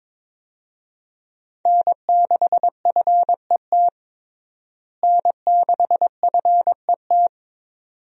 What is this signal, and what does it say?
Morse code, heard as a keyed tone, about 22 wpm, 700 Hz. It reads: N6FET N6FET
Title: USA Call Signs - 5 Characters - Repeated 2x - 22wpm